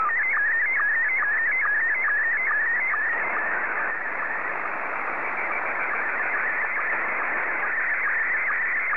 SSTV (slow-scan television), Scottie 1.